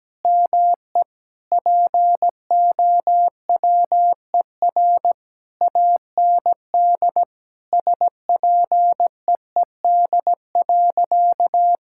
Morse code, heard as a keyed tone, about 17 wpm, 700 Hz. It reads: ME POWER AND SPEED.
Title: Sentences from Top 1000 Words - Review Part 5 - 17wpm